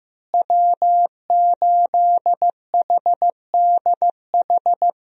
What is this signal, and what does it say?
Morse code, heard as a keyed tone, about 15 wpm, 700 Hz. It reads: W8HDH
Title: USA Call Signs - 5 Characters - Repeated 2x - 15wpm